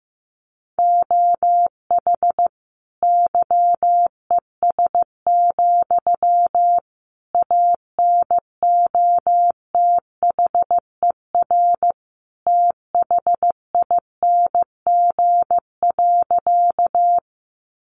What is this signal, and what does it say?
Morse code, heard as a keyed tone, about 15 wpm, 700 Hz. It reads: OH YES, ANOTHER THING.
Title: Sentences from Top 600 Words 15wpm